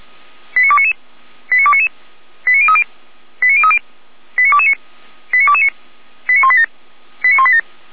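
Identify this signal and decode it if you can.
Five-tone selective call (ZVEI), heard as quick runs of five tones, tones of about 70 ms. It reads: 89299 89299 89030 89030 89229 89229 89187 89187
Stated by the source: ZVEI five-tone call received on SDR, fire brigade Orlamünde (feuerwehr-orlamuende) recording.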